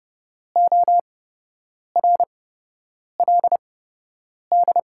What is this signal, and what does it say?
Morse code, heard as a keyed tone, about 30 wpm, spaced as for 8 wpm, 700 Hz. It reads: ORLD
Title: 8x Character Spacing - Top 300 Words - 30wpm